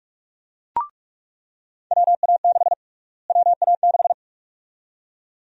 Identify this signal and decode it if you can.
Morse code, heard as a keyed tone, about 46 wpm, 700 Hz. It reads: WA6 WA6
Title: USA Call Sign Prefixes - Repeated 2x - 45wpm